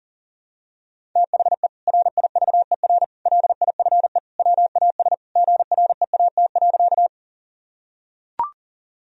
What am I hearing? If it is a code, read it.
Morse code, heard as a keyed tone, about 40 wpm, 700 Hz. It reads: THE RIVER LIFE WAS GREAT.
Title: Sentences from Top 1000 Words - Review Part 1 - 40wpm